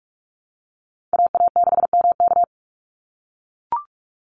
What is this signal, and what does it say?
Morse code, heard as a keyed tone, about 44 wpm, 700 Hz. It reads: AA6MK
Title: Call Signs - 45wpm